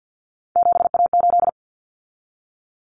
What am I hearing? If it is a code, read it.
Morse code, heard as a keyed tone, about 50 wpm, 700 Hz. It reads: 7A8
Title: Common Cut Numbers - 3-Digit 50wpm